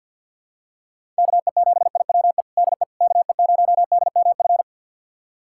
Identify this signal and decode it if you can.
Morse code, heard as a keyed tone, about 48 wpm, 700 Hz. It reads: KE7IWE DE KE0DMF
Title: USA Callsign DE Callsign 50wpm